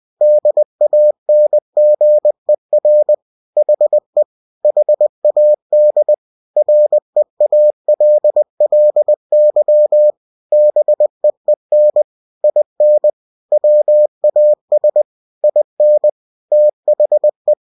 Morse code, heard as a keyed tone, about 20 wpm, 600 Hz. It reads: DANGER HE HAD REALLY BEEN IN WAS IN THE
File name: cw-books / winnie-the-pooh / winnie-20wpm-page-0100